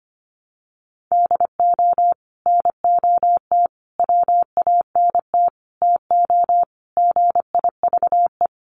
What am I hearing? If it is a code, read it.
Morse code, heard as a keyed tone, about 25 wpm, 700 Hz. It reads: DO NOT WANT TO GIVE
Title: Sets of 5 Words Encore 25wpm